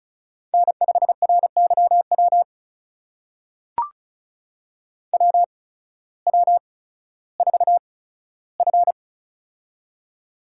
Morse code, heard as a keyed tone, about 36 wpm, spaced as for 16 wpm, 700 Hz. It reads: N5RYW WW4F
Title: USA Call Signs - 8x Character Spacing - 35wpm